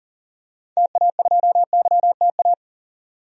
Morse code, heard as a keyed tone, about 40 wpm, 700 Hz. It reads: TA2YTA
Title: DXCC-100 Call Signs - 6 Characters - 40wpm